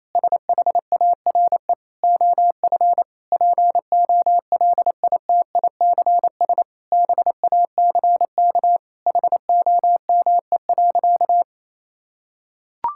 Morse code, heard as a keyed tone, about 28 wpm, 700 Hz. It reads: SHARE OF POLITICS BACK HOME.
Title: Sentences from Top 1000 Words - Review Part 6 - 28wpm